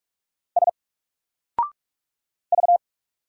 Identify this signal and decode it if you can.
Morse code, heard as a keyed tone, about 46 wpm, 700 Hz. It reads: S V
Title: Single Letter-Number 45wpm